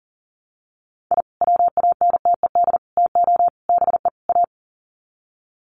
Morse code, heard as a keyed tone, about 40 wpm, 700 Hz. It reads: I WANTED TO BE A
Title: Sets of 5 Words Encore2 40wpm